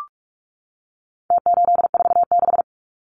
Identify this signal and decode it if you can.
Morse code, heard as a keyed tone, about 44 wpm, 700 Hz. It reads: T846
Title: Numbers - Contest Style - 4-Digit 45wpm